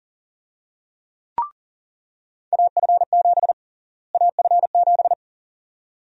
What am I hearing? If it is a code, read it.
Morse code, heard as a keyed tone, about 37 wpm, 700 Hz. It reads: AF7 AF7
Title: USA Call Sign Prefixes - Repeated 2x - 40wpm